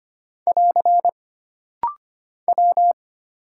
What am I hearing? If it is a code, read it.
Morse code, heard as a keyed tone, about 25 wpm, 700 Hz. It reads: <AR> W
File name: Single Character (Letter:Number:Punctuation:Prosign) - Rapid-Fire - 25wpm